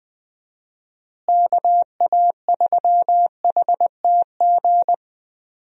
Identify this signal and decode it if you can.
Morse code, heard as a keyed tone, about 20 wpm, 700 Hz. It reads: KA3HTG
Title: USA Call Signs - 6 Characters - Repeated 3x - 20wpm